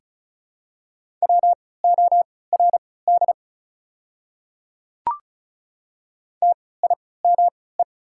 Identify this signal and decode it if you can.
Morse code, heard as a keyed tone, about 36 wpm, spaced as for 20 wpm, 700 Hz. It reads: WORD TIME
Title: 3x Character Spacing - Top 100 Words - 35wpm